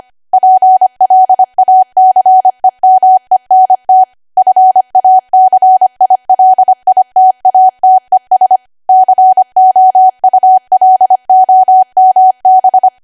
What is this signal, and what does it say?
Morse code, heard as a keyed tone, about 25 wpm, 750 Hz. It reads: PLACEMENT FACILITATES COULOMB
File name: Archive / 25 WPM / 25WPM